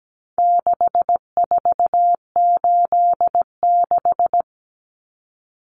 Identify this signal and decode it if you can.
Morse code, heard as a keyed tone, about 17 wpm, 700 Hz. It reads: 6486
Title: Numbers - 4-Digit 17wpm